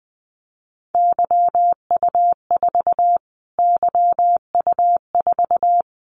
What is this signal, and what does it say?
Morse code, heard as a keyed tone, about 20 wpm, 700 Hz. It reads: YU4 YU4
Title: Callsign Prefixes for Top 25 DXCC - Repeated 3x - 20wpm